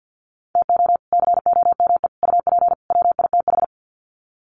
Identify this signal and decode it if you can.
Morse code, heard as a keyed tone, about 50 wpm, 700 Hz. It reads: TO COME UP WITH
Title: Sets of 4 Words 50wpm